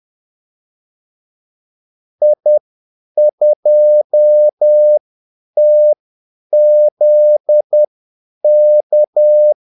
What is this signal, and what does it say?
Morse code, heard as a keyed tone, about 10 wpm, spaced as for 8 wpm, 600 Hz. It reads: I2TZK
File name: Callsigns-10-8wpm